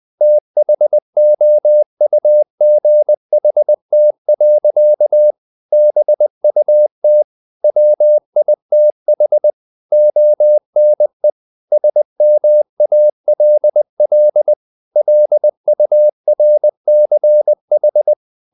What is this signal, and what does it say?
Morse code, heard as a keyed tone, about 20 wpm, 600 Hz. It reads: THOUGHT. BUT WITH ONE SMALL LURCH
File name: the old man and the sea part 2 20wpm